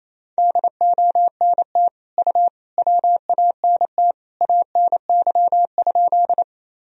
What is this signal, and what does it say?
Morse code, heard as a keyed tone, about 28 wpm, 700 Hz. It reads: DONT U WANT ANY?
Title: Sentences from Top 300 Words and Abbreviations 28wpm